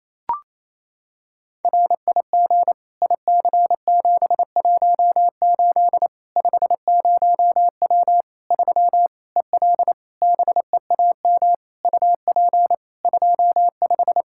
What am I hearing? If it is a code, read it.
Morse code, heard as a keyed tone, about 28 wpm, 700 Hz. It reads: RIG IC718 50W 3 EL BEAM UP 25
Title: Practice QSO - Exchange 2 - Info - 28wpm